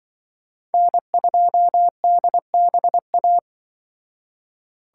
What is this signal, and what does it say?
Morse code, heard as a keyed tone, about 23 wpm, 700 Hz. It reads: N2DBA
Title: Speed Racing - USA 1x3 Call Signs 20wpm